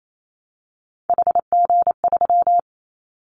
Morse code, heard as a keyed tone, about 28 wpm, 700 Hz. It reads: HG3
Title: Callsign Prefixes for Top 25 DXCC - 28wpm